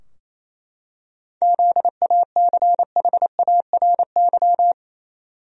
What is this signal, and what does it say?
Morse code, heard as a keyed tone, about 28 wpm, 700 Hz. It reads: ZACHARY
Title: USA Common Names - 28wpm